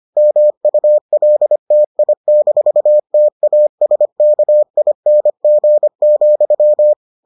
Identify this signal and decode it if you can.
Morse code, heard as a keyed tone, about 25 wpm, 600 Hz. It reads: MULTI-TASKING,